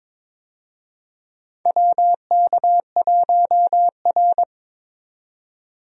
Morse code, heard as a keyed tone, about 22 wpm, 700 Hz. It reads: WK1R
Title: USA Call Signs - 4 Characters - 22wpm